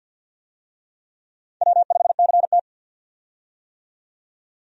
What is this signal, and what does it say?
Morse code, heard as a keyed tone, about 49 wpm, 700 Hz. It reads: W5CT
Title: USA Call Signs - 4 Characters - 50wpm